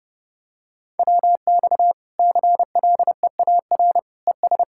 Morse code, heard as a keyed tone, about 30 wpm, 700 Hz. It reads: WX CLEAR ES
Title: Practice QSO - Exchange 2 - Info - 30wpm